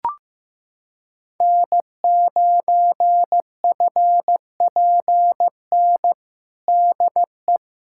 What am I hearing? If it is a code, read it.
Morse code, heard as a keyed tone, about 15 wpm, 700 Hz. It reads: N9FPN DE
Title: USA Callsign DE Callsign 15wpm